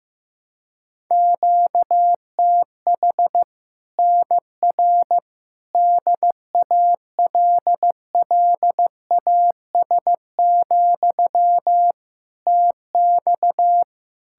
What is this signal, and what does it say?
Morse code, heard as a keyed tone, about 15 wpm, 700 Hz. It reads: QTH NR DALLAS, TX
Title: Practice QSO - Exchange 1 - Beginning - 15wpm